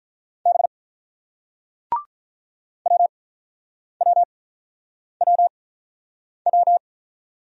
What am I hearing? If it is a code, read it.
Morse code, heard as a keyed tone, about 49 wpm, spaced as for 9 wpm, 690 Hz. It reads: B WWWW